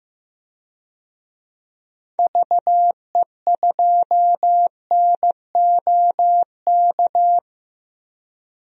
Morse code, heard as a keyed tone, about 15 wpm, 700 Hz. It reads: VE2NOK
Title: DXCC-100 Call Signs - 6 Characters - 15wpm